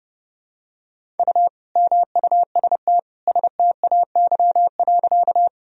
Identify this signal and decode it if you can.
Morse code, heard as a keyed tone, about 30 wpm, 700 Hz. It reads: U MUST STAY.
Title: Sentences from Top 500 Words and Abbreviations 30wpm